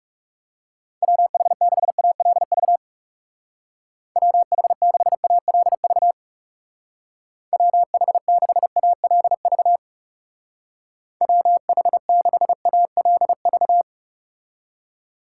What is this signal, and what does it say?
Morse code, heard as a keyed tone, about 36 wpm, 700 Hz. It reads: WH6ALV WH6ALV WH6ALV WH6ALV
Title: Speed Racing - USA 2x3 Call Signs 30wpm